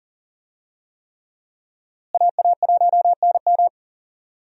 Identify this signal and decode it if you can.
Morse code, heard as a keyed tone, about 40 wpm, 700 Hz. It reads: AA1NM
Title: USA Call Signs - 5 Characters - Repeated 3x - 40wpm